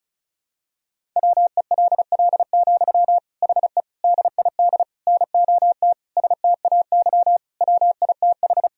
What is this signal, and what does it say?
Morse code, heard as a keyed tone, about 36 wpm, 700 Hz. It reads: WELL, HE DID NOT STAY WITH
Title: Sentences from Top 1000 Words - Review Part 2 - 35wpm